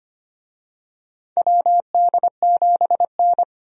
Morse code, heard as a keyed tone, about 25 wpm, 700 Hz. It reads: WD7N
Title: USA Call Signs - 4 Characters - 25wpm